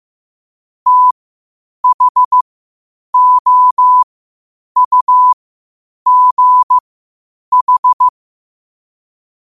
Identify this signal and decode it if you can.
Morse code, heard as a keyed tone, about 15 wpm, spaced as for 8 wpm, 1000 Hz. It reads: THOUGH
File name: qotdfiles / 8wpm